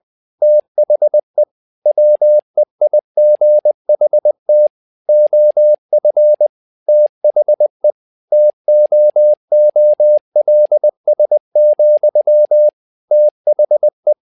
Morse code, heard as a keyed tone, about 20 wpm, 600 Hz. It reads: THE WEIGHT OF THE TOOLS, THE
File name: tower trouble 20wpm